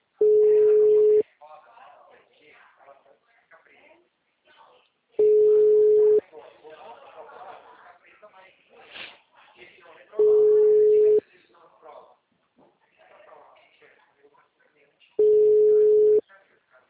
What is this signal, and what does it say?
Ringback tone (Europe, 425 Hz), with speech.